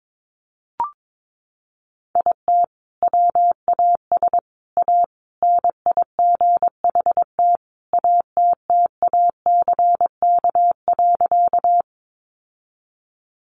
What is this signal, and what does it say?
Morse code, heard as a keyed tone, about 22 wpm, 700 Hz. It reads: IT WAS A NIGHT ATTACK.
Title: Sentences from Top 1500 Words 22wpm